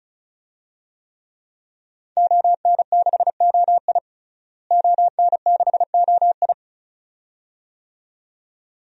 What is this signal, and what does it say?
Morse code, heard as a keyed tone, about 36 wpm, 700 Hz. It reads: ON6OI ON6OI
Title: DXCC-100 Call Signs - 5 Characters - Repeated 2x - 35wpm